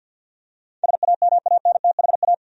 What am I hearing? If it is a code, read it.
Morse code, heard as a keyed tone, about 47 wpm, 700 Hz. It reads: SAMANTHA
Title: USA Common Names - 50wpm